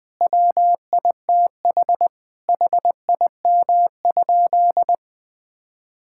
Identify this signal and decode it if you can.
Morse code, heard as a keyed tone, about 21 wpm, 700 Hz. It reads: WITH HIM?
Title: Sentences from Top 1000 Words - Review Part 6 - 20wpm